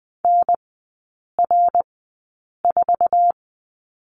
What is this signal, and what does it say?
Morse code, heard as a keyed tone, about 20 wpm, spaced as for 8 wpm, 700 Hz. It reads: NR4
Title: 053 Lesson 10e - Letter r - Three Characters 20wpm